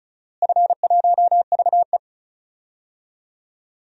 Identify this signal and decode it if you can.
Morse code, heard as a keyed tone, about 34 wpm, 700 Hz. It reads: F1VE
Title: DXCC-100 Call Signs - 4 Characters - Repeated 3x - 35wpm